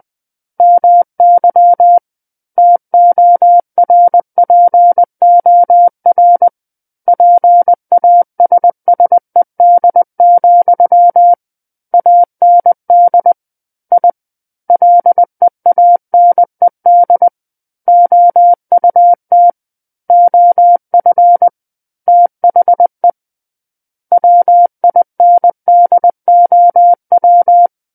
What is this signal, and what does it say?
Morse code, heard as a keyed tone, about 20 wpm, 700 Hz. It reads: MY TORPOR PASSED, AND I LEANED OUT OF THE WINDOW
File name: 20 WPM